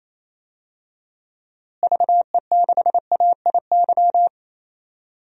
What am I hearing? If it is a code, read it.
Morse code, heard as a keyed tone, about 27 wpm, 700 Hz. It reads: VE6AIY
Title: DXCC-100 Call Signs - 6 Characters - Repeated 3x - 28wpm